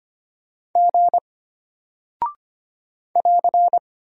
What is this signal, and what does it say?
Morse code, heard as a keyed tone, about 25 wpm, 700 Hz. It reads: G <AR>
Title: Single Character (Letter:Number:Punctuation:Prosign) Alt 25wpm